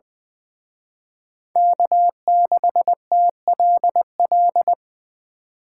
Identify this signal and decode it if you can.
Morse code, heard as a keyed tone, about 24 wpm, 700 Hz. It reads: K6TLL
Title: Speed Racing - USA Call Signs 20wpm